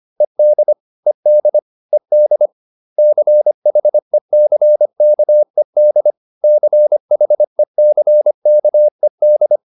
Morse code, heard as a keyed tone, about 25 wpm, 600 Hz. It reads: ED ED ED CHECKED CHECKED